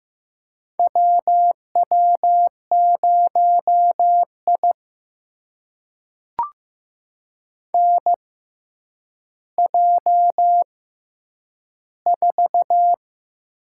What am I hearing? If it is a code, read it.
Morse code, heard as a keyed tone, about 15 wpm, spaced as for 8 wpm, 700 Hz. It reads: W W0I NJ4
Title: USA Call Signs - 4 Characters - 6x Character Spacing - 15wpm